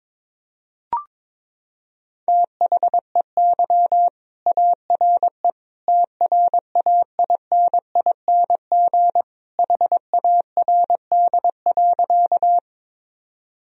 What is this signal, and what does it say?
Morse code, heard as a keyed tone, about 21 wpm, 700 Hz. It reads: THEY ARE TRAINING HARD.
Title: Sentences from Top 1000 Words - Review Part 4 - 22wpm